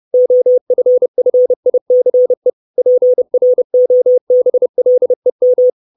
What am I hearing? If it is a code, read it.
Morse code, heard as a keyed tone, about 30 wpm, 500 Hz. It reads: OFFICE PROBLEM
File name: 632 Words 30WPM